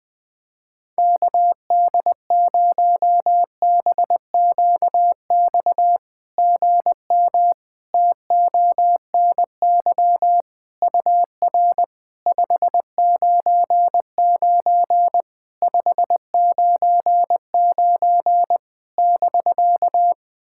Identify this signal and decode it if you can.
Morse code, heard as a keyed tone, about 20 wpm, 700 Hz. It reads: KD0BQX GM TONY UR 599 599 <BK>
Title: SOTA QSO Practice 20wpm